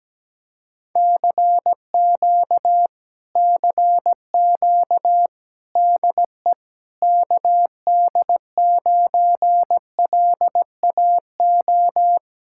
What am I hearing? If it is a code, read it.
Morse code, heard as a keyed tone, about 17 wpm, 700 Hz. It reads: CQ CQ DE KD9LAO
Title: Speed-Racing Common QSO Phrases Encore - 17wpm